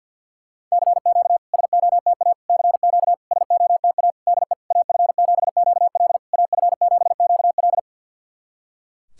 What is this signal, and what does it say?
Morse code, heard as a keyed tone, about 50 wpm, 700 Hz. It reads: CQ SOTA CQ SOTA DE AF7QL AF7QL
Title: SOTA QSO Practice 50wpm